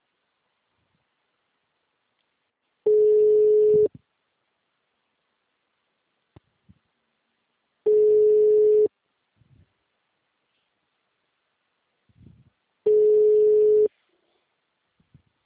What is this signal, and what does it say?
Ringback tone (Europe, 425 Hz).
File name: out- -2002- - - .8045